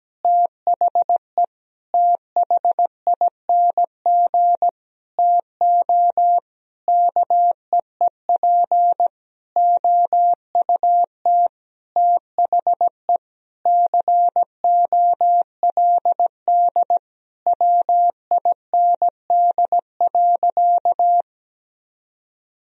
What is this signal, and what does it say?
Morse code, heard as a keyed tone, about 17 wpm, 700 Hz. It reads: THE THING TO KEEP OUT THE COLD WIND.
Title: Sentences from Top 1000 Words - Review Part 2 - 17wpm